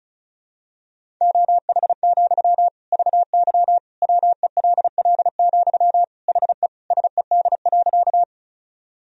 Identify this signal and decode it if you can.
Morse code, heard as a keyed tone, about 36 wpm, 700 Hz. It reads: OH, VY WELL, HE SED.
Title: Sentences from Top 500 Words and Abbreviations 35wpm